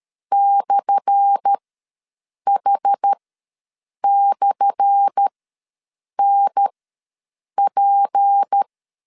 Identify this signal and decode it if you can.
Morse code, heard as a keyed tone, about 13 wpm, spaced as for 7 wpm, 800 Hz. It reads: /H/NP
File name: LEZIONI / 10 SIGNS / 10A SLASH ws 12wpm 3443